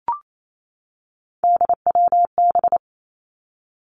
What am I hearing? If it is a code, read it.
Morse code, heard as a keyed tone, about 28 wpm, 700 Hz. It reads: DWB
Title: Three Challenging Letter-Number Combinations 28wpm